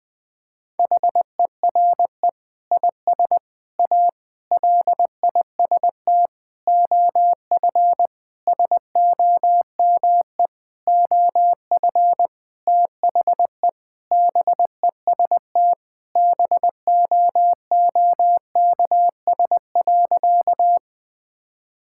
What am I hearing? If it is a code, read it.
Morse code, heard as a keyed tone, about 20 wpm, 700 Hz. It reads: HERE IS A LIST OF SOME OF THE BEST BOOKS.
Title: Sentences with 8-Word Phrases - 20wpm